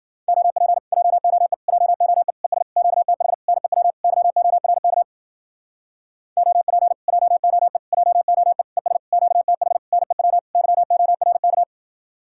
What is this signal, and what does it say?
Morse code, heard as a keyed tone, about 51 wpm, 700 Hz. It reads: OP JOE JOE ES QTH NEW YORK OP JOE JOE ES QTH NEW YORK